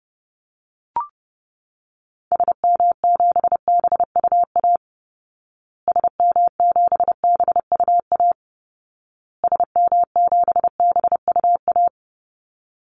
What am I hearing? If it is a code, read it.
Morse code, heard as a keyed tone, about 30 wpm, 700 Hz. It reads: SM7BUA SM7BUA SM7BUA
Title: Call Signs Repeated 2x - 30wpm